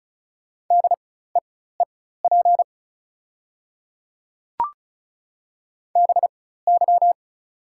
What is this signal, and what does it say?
Morse code, heard as a keyed tone, about 36 wpm, spaced as for 16 wpm, 700 Hz. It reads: DEEP BY
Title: 4x Character Spacing - Top 500 Words - 35wpm